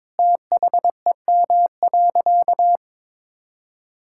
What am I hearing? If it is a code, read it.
Morse code, heard as a keyed tone, about 22 wpm, 700 Hz. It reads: THEM.
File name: Sentences with 7-Word Phrases - 22wpm